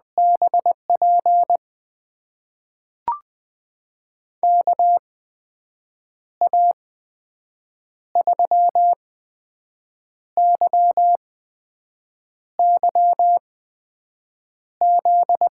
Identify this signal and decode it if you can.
Morse code, heard as a keyed tone, about 20 wpm, spaced as for 6 wpm, 700 Hz. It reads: BP KA3YYZ
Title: USA Call Signs - 6 Characters - 8x Character Spacing - 20wpm